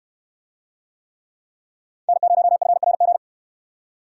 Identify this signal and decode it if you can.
Morse code, heard as a keyed tone, about 66 wpm, 700 Hz. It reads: N0FRG